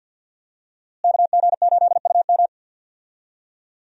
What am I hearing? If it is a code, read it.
Morse code, heard as a keyed tone, about 53 wpm, 700 Hz. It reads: KG8UM
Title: USA Call Signs - 5 Characters - Repeated 2x - 50wpm